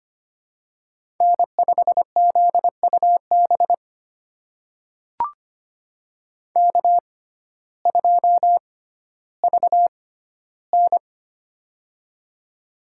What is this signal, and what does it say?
Morse code, heard as a keyed tone, about 25 wpm, spaced as for 14 wpm, 700 Hz. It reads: N5ZUB K2VN
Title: USA Call Signs - 6x Character Spacing - 25wpm